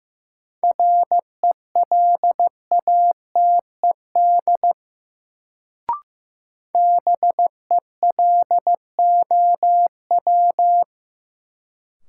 Morse code, heard as a keyed tone, about 16 wpm, 700 Hz. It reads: RELATED BELOW
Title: Sentences from Top 1000 Words - Review Part 5 - New Word Drill - 15wpm